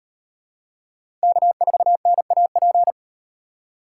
Morse code, heard as a keyed tone, about 36 wpm, 700 Hz. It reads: K4NAP